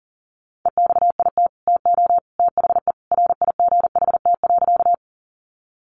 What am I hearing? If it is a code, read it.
Morse code, heard as a keyed tone, about 40 wpm, 700 Hz. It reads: EXIT TO THE RIGHT.
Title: Sentences from Top 1100 Words 40wpm